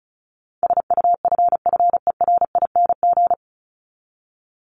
Morse code, heard as a keyed tone, about 34 wpm, 700 Hz. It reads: SUFFERING
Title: ING Suffix - Top 100 Words with Base Words 35wpm